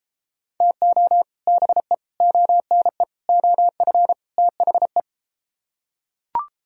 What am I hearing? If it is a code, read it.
Morse code, heard as a keyed tone, about 33 wpm, 700 Hz. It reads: TO BE ONE OF THE